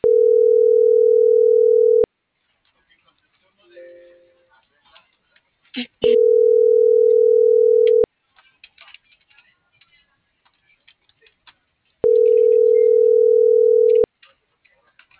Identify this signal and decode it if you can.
Ringback tone (North America), 440 + 480 Hz.